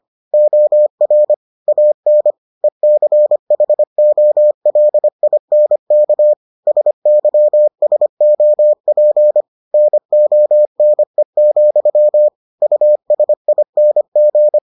Morse code, heard as a keyed tone, about 25 wpm, 600 Hz. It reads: OR AN ECHOLINK SYSOP NONE, USING